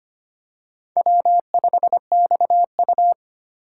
Morse code, heard as a keyed tone, about 25 wpm, 700 Hz. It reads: W5XU
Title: K1USN SST QSO Practice - 25wpm